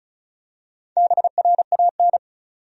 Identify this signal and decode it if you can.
Morse code, heard as a keyed tone, about 34 wpm, 700 Hz. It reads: BRAN
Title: USA Common Names - Max 4 Letters - 35wpm